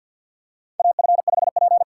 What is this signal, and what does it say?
Morse code, heard as a keyed tone, about 52 wpm, 700 Hz. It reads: AF5P
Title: K1USN SST QSO Practice - 50wpm